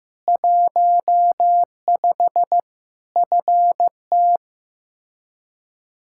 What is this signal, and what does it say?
Morse code, heard as a keyed tone, about 15 wpm, 700 Hz. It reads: 15 FT
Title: K1USN SST QSO Practice 15wpm